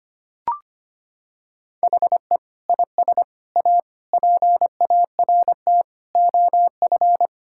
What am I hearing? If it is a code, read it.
Morse code, heard as a keyed tone, about 25 wpm, 700 Hz. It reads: HE IS A PART OF
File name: practice / 25wpm / Sentences with 4-Word Phrases - 25wpm